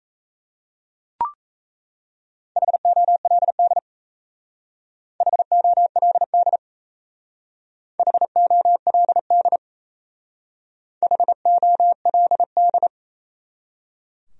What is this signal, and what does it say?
Morse code, heard as a keyed tone, about 33 wpm, 700 Hz. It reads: HOLD HOLD HOLD HOLD
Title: Speed-Racing Top 500 Words 28wpm